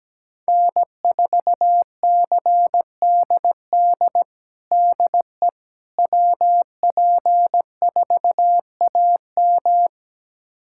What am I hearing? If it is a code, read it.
Morse code, heard as a keyed tone, about 17 wpm, 700 Hz. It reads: N4CDD DE WP4AM
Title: USA Callsign DE Callsign - 5 Characters - 17wpm